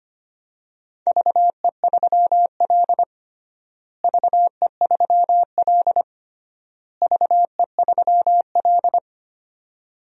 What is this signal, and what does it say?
Morse code, heard as a keyed tone, about 25 wpm, 700 Hz. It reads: VE3L VE3L VE3L
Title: DXCC-100 Call Signs - 4 Characters - Repeated 3x - 25wpm